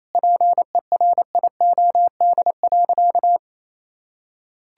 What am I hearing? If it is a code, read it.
Morse code, heard as a keyed tone, about 28 wpm, 700 Hz. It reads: PERIOD.
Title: Sentences from Top 1000 Words - Review Part 5 - 28wpm